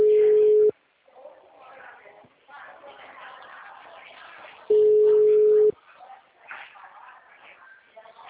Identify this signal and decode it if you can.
Ringback tone (Europe, 425 Hz), with speech.